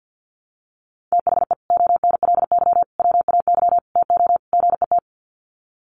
Morse code, heard as a keyed tone, about 50 wpm, 700 Hz. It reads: THE ONLY WAY TO GET
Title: Sets of 5 Words Encore2 50wpm